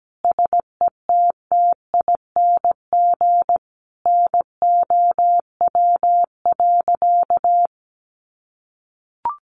Morse code, heard as a keyed tone, about 17 wpm, 700 Hz. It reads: SETTING NOW.
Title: Sentences from Top 1300 Words 17wpm